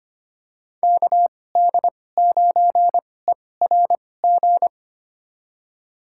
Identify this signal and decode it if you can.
Morse code, heard as a keyed tone, about 25 wpm, spaced as for 18 wpm, 700 Hz. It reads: KD9ERG
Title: USA Call Signs - 6 Characters - 2x Character Spacing - 25wpm